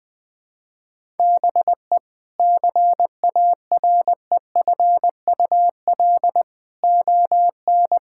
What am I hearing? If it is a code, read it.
Morse code, heard as a keyed tone, about 20 wpm, 700 Hz. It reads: BE CAREFUL ON
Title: Sentences from Top 1600 Words 20wpm